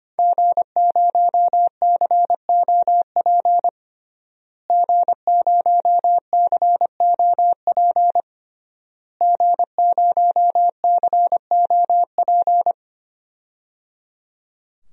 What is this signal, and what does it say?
Morse code, heard as a keyed tone, about 25 wpm, 700 Hz. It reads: G0COP G0COP G0COP
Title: DXCC-100 Call Signs - 5 Characters - Repeated 3x - 25wpm